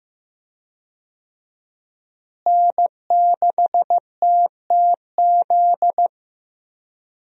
Morse code, heard as a keyed tone, about 15 wpm, 700 Hz. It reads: N6TTZ
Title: USA Call Signs - 5 Characters - Repeated 3x - 15wpm